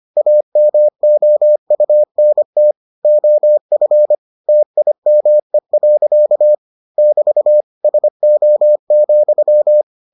Morse code, heard as a keyed tone, about 25 wpm, 600 Hz. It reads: AMOUNT OF TIME. <BT> SO,